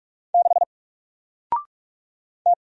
Morse code, heard as a keyed tone, about 44 wpm, 700 Hz. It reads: 6 T
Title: Single Character (Letter:Number:Punctuation:Prosign) Alt 45wpm